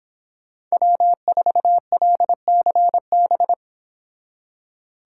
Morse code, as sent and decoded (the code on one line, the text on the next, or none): .-- ....- .-.. -.-. -...
W4LCB